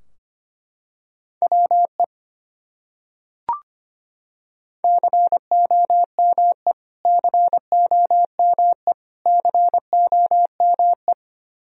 Morse code, heard as a keyed tone, about 25 wpm, 700 Hz. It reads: WE COME COME COME
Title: Top 100 Words Repeated 25wpm